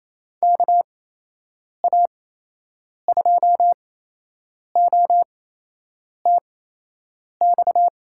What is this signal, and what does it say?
Morse code, heard as a keyed tone, about 28 wpm, spaced as for 8 wpm, 700 Hz. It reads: KA2OTX
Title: USA Call Signs - 6 Characters - 8x Character Spacing - 28wpm